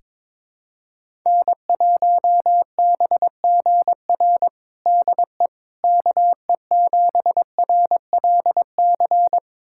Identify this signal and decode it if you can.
Morse code, heard as a keyed tone, about 22 wpm, 700 Hz. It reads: N1BGR DE KE7RLC
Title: USA Callsign DE Callsign 22wpm